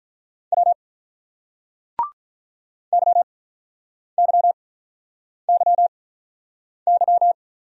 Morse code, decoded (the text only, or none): W Y Y Y Y